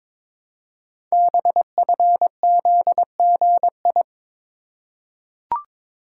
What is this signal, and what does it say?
Morse code, heard as a keyed tone, about 22 wpm, 700 Hz. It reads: BFZGI
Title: Code Groups - 5 Letters - 22wpm